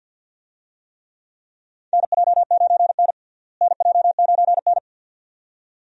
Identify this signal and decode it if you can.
Morse code, heard as a keyed tone, about 46 wpm, 700 Hz. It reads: NJ9N NJ9N
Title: USA Call Signs - 4 Characters - Repeated 2x - 50wpm